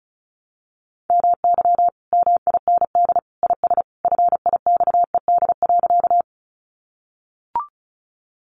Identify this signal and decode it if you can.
Morse code, heard as a keyed tone, about 34 wpm, 700 Hz. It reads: MY MIND IS FIXED.
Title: Sentences from Top 1100 Words 35wpm